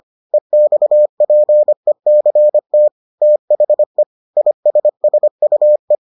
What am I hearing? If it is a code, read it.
Morse code, heard as a keyed tone, about 25 wpm, 600 Hz. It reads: EXPECT THE ISSUE